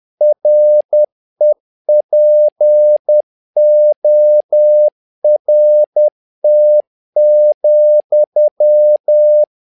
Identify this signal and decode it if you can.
Morse code, heard as a keyed tone, about 10 wpm, 600 Hz. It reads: REPORT,
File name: learning center / tower trouble / tower trouble 10wpm